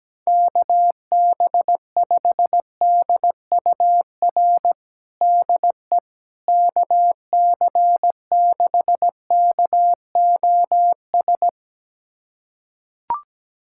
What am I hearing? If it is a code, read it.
Morse code, heard as a keyed tone, about 17 wpm, 700 Hz. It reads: KB5DUR DE KC6KOS
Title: USA Callsign DE Callsign - 6 Characters - 17wpm